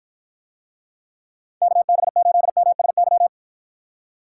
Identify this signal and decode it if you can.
Morse code, heard as a keyed tone, about 53 wpm, 700 Hz. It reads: KB8MSY